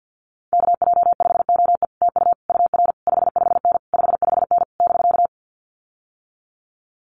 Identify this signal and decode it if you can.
Morse code, heard as a keyed tone, about 50 wpm, 700 Hz. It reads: KJ5OE TU UR 55N 55N <BK>
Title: Common QSO Phrases Encore 50wpm